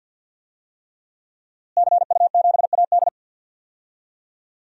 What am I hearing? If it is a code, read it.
Morse code, heard as a keyed tone, about 51 wpm, 700 Hz. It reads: CU7AD
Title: DXCC-100 Call Signs - 5 Characters - Repeated 3x - 50wpm